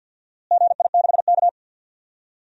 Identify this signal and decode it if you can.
Morse code, heard as a keyed tone, about 53 wpm, 700 Hz. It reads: GI6K
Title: DXCC-100 Call Signs - 4 Characters - Repeated 2x - 50wpm